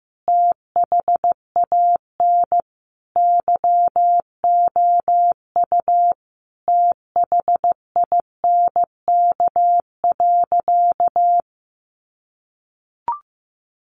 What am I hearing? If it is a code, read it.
Morse code, heard as a keyed tone, about 15 wpm, 700 Hz. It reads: THAN YOU THINK.
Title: Sentences from Top 600 Words 15wpm